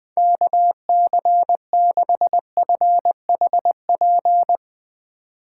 Morse code, heard as a keyed tone, about 20 wpm, 700 Hz. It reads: KC6FHP
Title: Speed Racing - USA 2x3 Call Signs 17wpm